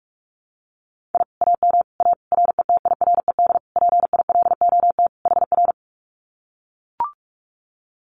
Morse code, heard as a keyed tone, about 44 wpm, 700 Hz. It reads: I AM A RETIRED PILOT HR
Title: Common QSO Phrases Encore 45wpm